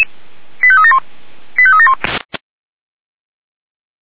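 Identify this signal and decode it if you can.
Five-tone selective call (ZVEI), heard as quick runs of five tones, tones of about 70 ms. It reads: 85371 85371
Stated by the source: ZVEI five-tone call 85381 received by SDR.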